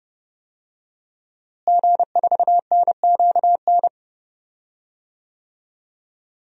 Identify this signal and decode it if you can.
Morse code, heard as a keyed tone, about 32 wpm, 700 Hz. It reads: G4NQN
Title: DXCC-100 Call Signs - 5 Characters - Repeated 3x - 30wpm